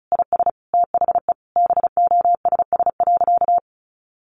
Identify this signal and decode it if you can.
Morse code, heard as a keyed tone, about 34 wpm, 700 Hz. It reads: IS THE BOSS.
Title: Sentences from Top 700 Words 35wpm